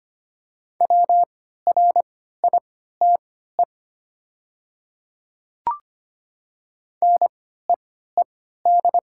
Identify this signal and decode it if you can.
Morse code, heard as a keyed tone, about 25 wpm, spaced as for 14 wpm, 700 Hz. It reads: WRITE NEED
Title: 3x Character Spacing - Top 200 Words - 25wpm